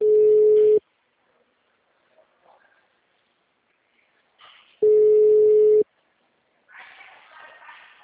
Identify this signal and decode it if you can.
Ringback tone (Europe, 425 Hz).
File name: out- -2021- - - .37455